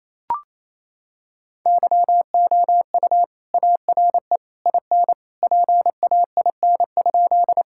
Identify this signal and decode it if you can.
Morse code, heard as a keyed tone, about 28 wpm, 700 Hz. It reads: YOU ARE IN PAIN?
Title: Sentences from Top 1600 Words 28wpm